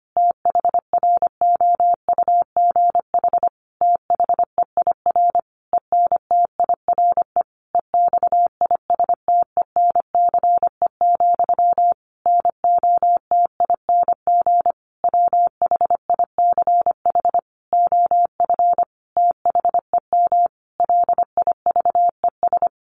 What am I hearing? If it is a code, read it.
Morse code, heard as a keyed tone, about 25 wpm, 700 Hz. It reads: THROUGH THEIR ENTIRE EXISTENCE, NOTING WHICH OF THEM LIVES
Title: A Sound Of Thunder 25wpm